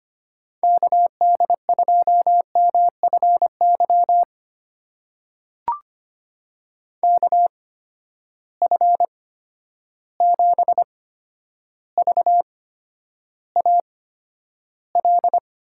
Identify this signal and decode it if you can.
Morse code, heard as a keyed tone, about 25 wpm, spaced as for 11 wpm, 700 Hz. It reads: KD2MFY KF7VAL
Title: USA Call Signs - 6 Characters - 8x Character Spacing - 25wpm